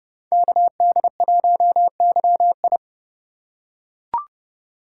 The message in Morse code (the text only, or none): KD1YI